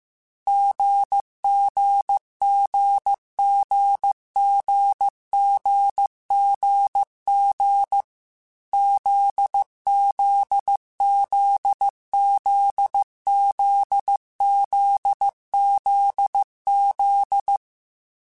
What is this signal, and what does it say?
Morse code, heard as a keyed tone, about 15 wpm, 800 Hz. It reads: GGGGGGGG ZZZZZZZZ
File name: Short Runs / Short Runs-mp3s / GZ GZ7 @13